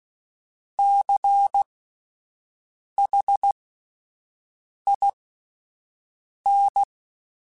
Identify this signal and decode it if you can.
Morse code, heard as a keyed tone, about 16 wpm, spaced as for 5 wpm, 800 Hz. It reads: CHIN
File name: Alphabetical / Alphabetical mp3s / Cs @5